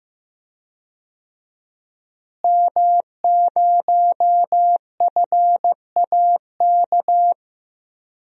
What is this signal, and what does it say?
Morse code, heard as a keyed tone, about 16 wpm, 700 Hz. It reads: M0FAK
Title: DXCC-100 Call Signs - 5 Characters - Repeated 3x - 15wpm